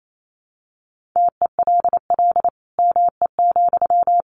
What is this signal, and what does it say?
Morse code, heard as a keyed tone, about 28 wpm, 700 Hz. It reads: TELL ME,
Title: Sentences from Top 1500 Words 28wpm